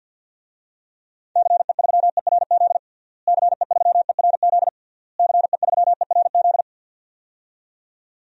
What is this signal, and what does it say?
Morse code, heard as a keyed tone, about 52 wpm, 700 Hz. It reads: CE3ERZ CE3ERZ CE3ERZ
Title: DXCC-100 Call Signs - 6 Characters - Repeated 3x - 50wpm